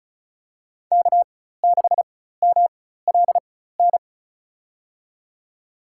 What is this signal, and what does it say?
Morse code, heard as a keyed tone, about 34 wpm, spaced as for 16 wpm, 700 Hz. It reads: K6MLN
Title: USA Call Signs - 5 Characters - 4x Character Spacing - 35wpm